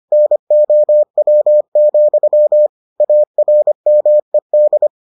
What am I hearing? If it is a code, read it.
Morse code, heard as a keyed tone, about 25 wpm, 600 Hz. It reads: NOW, ARMED